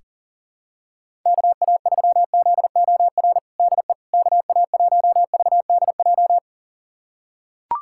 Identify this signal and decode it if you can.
Morse code, heard as a keyed tone, about 40 wpm, 700 Hz. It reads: KA3ZOR DE KA1VDJ
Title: USA Callsign DE Callsign - 6 Characters - 40wpm